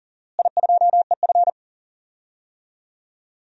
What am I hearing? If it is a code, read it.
Morse code, heard as a keyed tone, about 40 wpm, 700 Hz. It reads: I2EF
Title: DXCC-100 Call Signs - 4 Characters - Repeated 3x - 40wpm